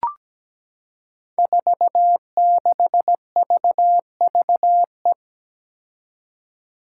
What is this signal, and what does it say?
Morse code, heard as a keyed tone, about 17 wpm, 700 Hz. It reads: 46VVE
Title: Five Challenging Letter-Number Combinations 17wpm